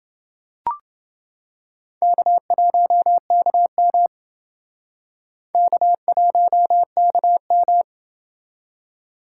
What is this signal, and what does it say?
Morse code, heard as a keyed tone, about 29 wpm, 700 Hz. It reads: K1KM K1KM